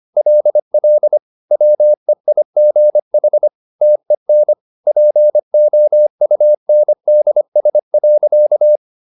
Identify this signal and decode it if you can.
Morse code, heard as a keyed tone, about 25 wpm, 600 Hz. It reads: LL WEIGH TEN POUNDS.
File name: the old man and the sea part 1 25wpm